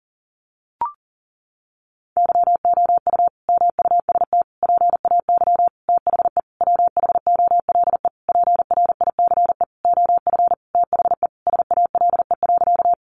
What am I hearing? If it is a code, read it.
Morse code, heard as a keyed tone, about 40 wpm, 700 Hz. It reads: YOU MUST PAY THE WHOLE PRICE OF THE SALE.
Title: Sentences from Top 1400 Words 40wpm